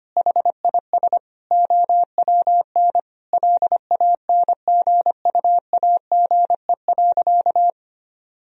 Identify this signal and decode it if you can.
Morse code, heard as a keyed tone, about 25 wpm, 700 Hz. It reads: HIS OWN LANGUAGE.
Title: Sentences from Top 1000 Words - Review Part 3 - 25wpm